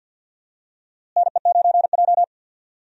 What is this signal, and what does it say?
Morse code, heard as a keyed tone, about 52 wpm, 700 Hz. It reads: NE9J
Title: USA Call Signs - 4 Characters - Repeated 3x - 50wpm